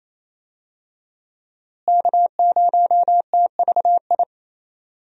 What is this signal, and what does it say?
Morse code, heard as a keyed tone, about 27 wpm, 700 Hz. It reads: K0TVI
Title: USA Call Signs - 5 Characters - Repeated 3x - 28wpm